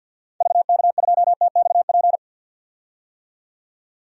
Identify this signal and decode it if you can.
Morse code, heard as a keyed tone, about 51 wpm, 700 Hz. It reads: VK2TXP
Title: DXCC-100 Call Signs - 6 Characters - 50wpm